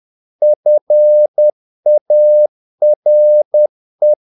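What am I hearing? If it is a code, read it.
Morse code, heard as a keyed tone, about 10 wpm, 600 Hz. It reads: FARE
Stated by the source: Morse practice of short Danish words at 10 wpm, from CW med Gnister.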